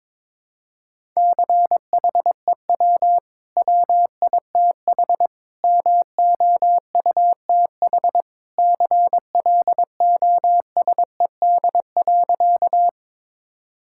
Morse code, heard as a keyed tone, about 22 wpm, 690 Hz. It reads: CHEW WITH MOUTH CLOSED.
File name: practice / 22wpm / Easy Sentences with Hints - 22wpm